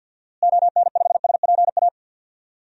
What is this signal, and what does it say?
Morse code, heard as a keyed tone, about 47 wpm, 700 Hz. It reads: ON5SPA
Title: DXCC-100 Call Signs - 6 Characters - Repeated 2x - 50wpm